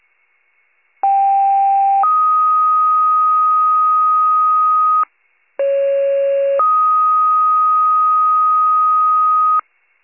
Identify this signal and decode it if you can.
Two-tone page, heard as a one-second tone, then a three-second tone, about 790 then 1250 Hz, about 550 then 1250 Hz. It is heard from 1.0 to 9.6 s.